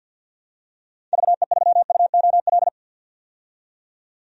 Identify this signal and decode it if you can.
Morse code, heard as a keyed tone, about 47 wpm, 700 Hz. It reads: VE3UOL
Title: DXCC-100 Call Signs - 6 Characters - 50wpm